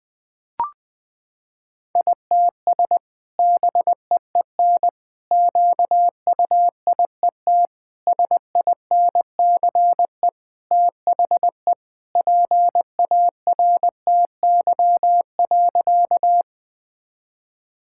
Morse code, heard as a keyed tone, about 21 wpm, 700 Hz. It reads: ITS BEEN QUIET SINCE THE PARTY.
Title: Sentences with 2-Word Phrases - 20wpm